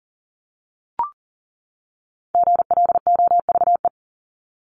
Morse code, heard as a keyed tone, about 40 wpm, 700 Hz. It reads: GLOVE
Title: Top 1751-2000 Words 40wpm